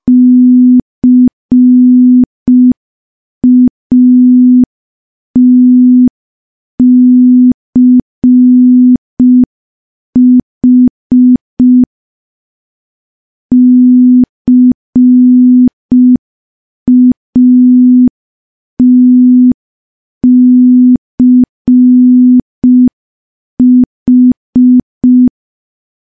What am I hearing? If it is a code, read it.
Morse code, heard as a keyed tone, about 5 wpm, 250 Hz. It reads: CATCH CATCH
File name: morse / morse-training-05WPM